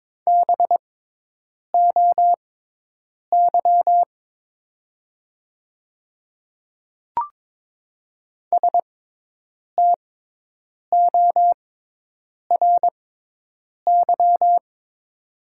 Morse code, heard as a keyed tone, about 23 wpm, spaced as for 8 wpm, 700 Hz. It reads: BOY STORY
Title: 6x Character Spacing - Top 500 Words - 22wpm